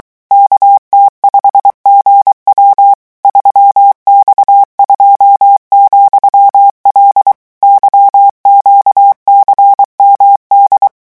Morse code, heard as a keyed tone, about 23 wpm, 800 Hz. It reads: KT5GW 3X2,L YQCMD
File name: morse mp3 / revision13-vitesse 22 mots